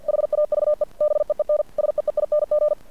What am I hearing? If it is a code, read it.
Morse code, heard as a keyed tone, about 50 wpm, 600 Hz. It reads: HAVE BEEN SEEING
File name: archived / CRSnet